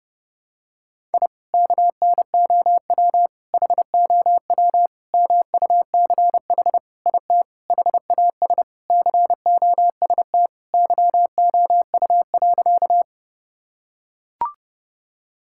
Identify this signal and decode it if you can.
Morse code, heard as a keyed tone, about 30 wpm, 700 Hz. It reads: I KNOW HOW MUCH IT HAS COST YOU.
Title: Sentences from Top 1000 Words - Review Part 5 - 30wpm